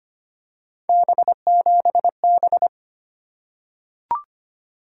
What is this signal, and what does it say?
Morse code, heard as a keyed tone, about 25 wpm, 700 Hz. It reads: B7B
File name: Three Letter-Number Combinations - 25wpm